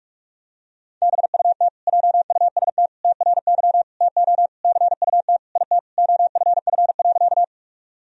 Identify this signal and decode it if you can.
Morse code, heard as a keyed tone, about 46 wpm, 700 Hz. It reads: BUT JUST TRY TO CUT IT OFF.
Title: Sentences from Top 1000 Words - Review Part 1 - 45wpm